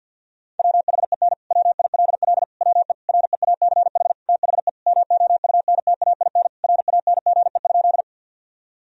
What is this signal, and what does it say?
Morse code, heard as a keyed tone, about 50 wpm, 700 Hz. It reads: WHEN WILL WE REACH THE MOUNTAIN RANGE?
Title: Sentences from Top 1000 Words - Review Part 4 - 50wpm